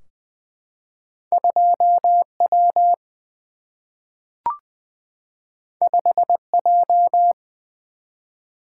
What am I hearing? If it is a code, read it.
Morse code, heard as a keyed tone, about 20 wpm, 700 Hz. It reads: 2 W 5 J
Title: Single-Letter Single-Number 20wpm